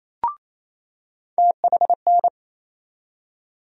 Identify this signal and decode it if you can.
Morse code, heard as a keyed tone, about 27 wpm, 700 Hz. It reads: THN
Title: USA Call Sign Postfixes - 3 Letters 28wpm